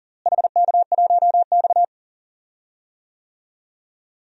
Morse code, heard as a keyed tone, about 38 wpm, 700 Hz. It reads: HK1X
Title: DXCC-100 Call Signs - 4 Characters - Repeated 3x - 40wpm